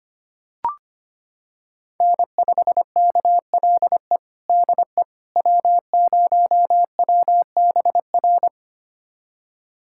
Morse code, heard as a keyed tone, about 25 wpm, 700 Hz. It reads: N5KLE DE W0WBR
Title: USA Callsign DE Callsign - 5 Characters - 25wpm